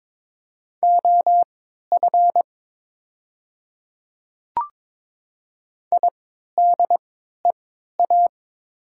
Morse code, heard as a keyed tone, about 22 wpm, spaced as for 12 wpm, 700 Hz. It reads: OF IDEA
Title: 3x Character Spacing - Top 300 Words - 22wpm